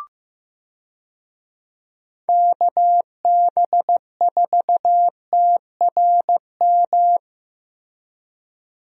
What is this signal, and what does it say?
Morse code, heard as a keyed tone, about 15 wpm, 700 Hz. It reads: KB4TRM
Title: USA Call Signs - 6 Characters - Repeated 2x - 15wpm